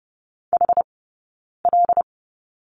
Morse code, heard as a keyed tone, about 30 wpm, spaced as for 9 wpm, 690 Hz. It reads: HL
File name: Lesson 19d - Letter w - Two Characters 30wpm